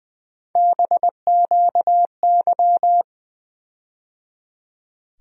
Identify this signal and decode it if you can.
Morse code, heard as a keyed tone, about 21 wpm, 700 Hz. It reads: BQY
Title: USA Call Sign Postfixes - 3 Letters 20wpm